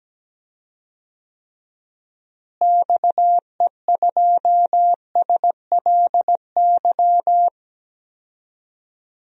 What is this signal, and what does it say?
Morse code, heard as a keyed tone, about 17 wpm, 700 Hz. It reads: XE2SLY
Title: DXCC-100 Call Signs - 6 Characters - Repeated 3x - 17wpm